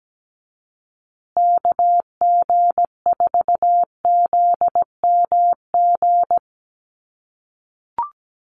Morse code, heard as a keyed tone, about 17 wpm, 700 Hz. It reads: KG4ZMG
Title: USA Call Signs Repeated 2x 17wpm